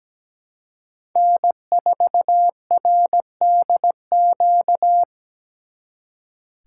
Morse code, heard as a keyed tone, about 20 wpm, 700 Hz. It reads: N4RDQ
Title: Speed Racing - USA Call Signs 17wpm